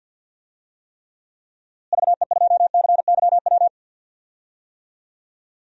Morse code, heard as a keyed tone, about 51 wpm, 700 Hz. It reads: VE2CYW
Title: DXCC-100 Call Signs - 6 Characters - Repeated 3x - 50wpm